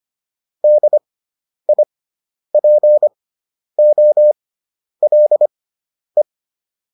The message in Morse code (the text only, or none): DIPOLE